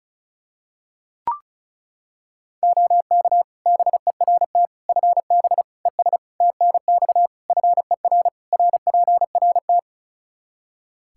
Morse code, heard as a keyed tone, about 36 wpm, 700 Hz. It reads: OK BERT FB ES TNX FER RPRT
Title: Practice QSO - Exchange 2 - Info - 35wpm